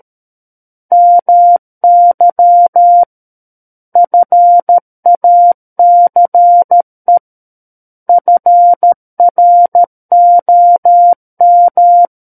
Morse code, heard as a keyed tone, about 13 wpm, 700 Hz. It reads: MY FACE FROM
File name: Book One / 04-Chapter 4 - 13 WPM